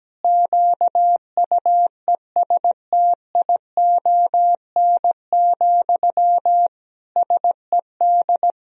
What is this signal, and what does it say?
Morse code, heard as a keyed tone, about 17 wpm, 700 Hz. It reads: QUESTION, SED
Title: Sentences from Top 400 Words and Abbreviations 17wpm